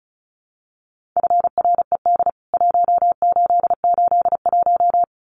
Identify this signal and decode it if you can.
Morse code, heard as a keyed tone, about 34 wpm, 700 Hz. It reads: FRED 1881
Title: CWT Exchange 35wpm